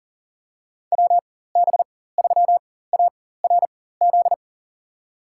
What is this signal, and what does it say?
Morse code, heard as a keyed tone, about 39 wpm, spaced as for 19 wpm, 700 Hz. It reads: WB3ARZ
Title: USA Call Signs - 6 Characters - 4x Character Spacing - 40wpm